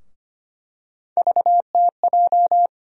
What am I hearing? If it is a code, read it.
Morse code, heard as a keyed tone, about 25 wpm, 700 Hz. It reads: VTJ
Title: USA Call Sign Postfixes - 3 Letters 25wpm